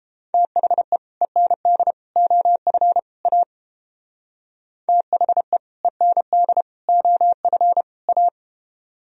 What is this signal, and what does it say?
Morse code, heard as a keyed tone, about 31 wpm, 700 Hz. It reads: THE END OF A THE END OF A